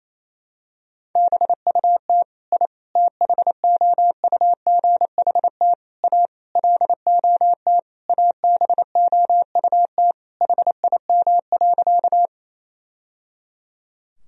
Morse code, heard as a keyed tone, about 28 wpm, 700 Hz. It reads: BUT I THOUGHT A LOT ABOUT HIM.
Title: Sentences from Top 1000 Words - Review Part 3 - 28wpm